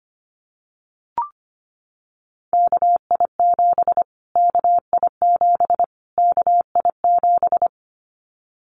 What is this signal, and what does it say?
Morse code, heard as a keyed tone, about 25 wpm, 700 Hz. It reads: KI7 KI7 KI7
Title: Callsign Prefixes for Top 25 DXCC - Repeated 3x - 25wpm